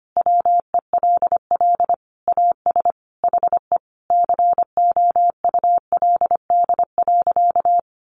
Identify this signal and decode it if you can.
Morse code, heard as a keyed tone, about 25 wpm, 700 Hz. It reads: WELL AS HE COULD.
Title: Sentences from Top 500 Words 25wpm